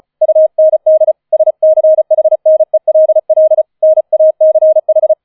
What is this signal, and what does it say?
Morse code, heard as a keyed tone, about 34 wpm, 630 Hz. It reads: UND SCHNELL NACH
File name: 004 Vierter Streich 35wpm